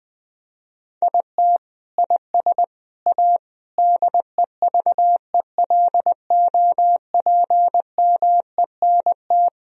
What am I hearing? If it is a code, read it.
Morse code, heard as a keyed tone, about 20 wpm, 700 Hz. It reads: IT IS A DEVELOPMENT
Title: Sentences from Top 1000 Words - Review Part 4 - 20wpm